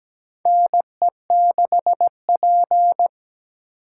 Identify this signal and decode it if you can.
Morse code, heard as a keyed tone, about 17 wpm, 700 Hz. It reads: NE6P
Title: USA Call Signs - 4 Characters - 8x Character Spacing - 17wpm